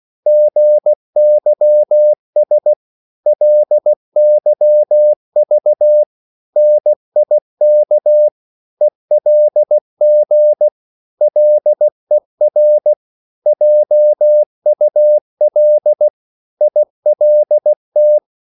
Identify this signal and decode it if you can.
Morse code, heard as a keyed tone, about 16 wpm, 600 Hz. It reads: GYS LYV NIK ELG LER JUL ILT
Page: Korte danske ord 16wpm